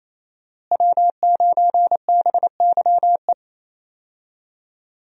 Morse code, heard as a keyed tone, about 28 wpm, 700 Hz. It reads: W9BYE
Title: USA Call Signs - 5 Characters - Repeated 3x - 28wpm